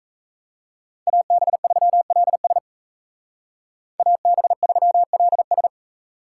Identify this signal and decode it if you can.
Morse code, heard as a keyed tone, about 36 wpm, 700 Hz. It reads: AB3LS AB3LS